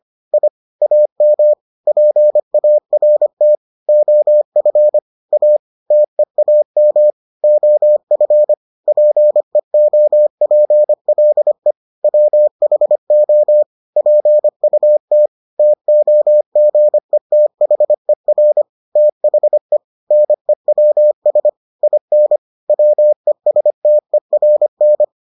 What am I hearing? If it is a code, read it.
Morse code, heard as a keyed tone, about 25 wpm, 600 Hz. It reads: I AM PART OF A TEAM OF PEOPLE WHO PUT TOGETHER THE NEWS IN WESTERN